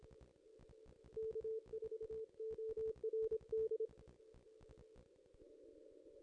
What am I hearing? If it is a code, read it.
Morse code, heard as a keyed tone, about 25 wpm, 440 Hz. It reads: K4ORD
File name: contests / ss-cw-k5tr-op-n5rz -1600